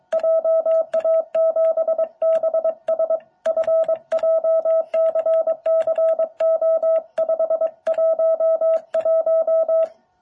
Morse code, heard as a keyed tone, about 22 wpm, 650 Hz. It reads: JA7BSF JCCO511
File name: jh8cbh / 2021cw